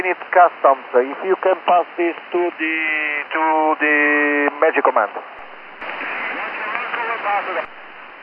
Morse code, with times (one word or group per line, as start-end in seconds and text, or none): none